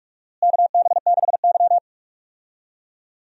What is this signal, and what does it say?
Morse code, heard as a keyed tone, about 46 wpm, 700 Hz. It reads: KB6Y
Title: USA Call Signs - 4 Characters - Repeated 3x - 45wpm